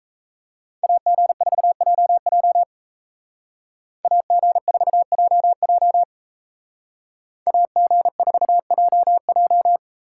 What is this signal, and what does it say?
Morse code, heard as a keyed tone, about 36 wpm, 700 Hz. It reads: AG4JJ AG4JJ AG4JJ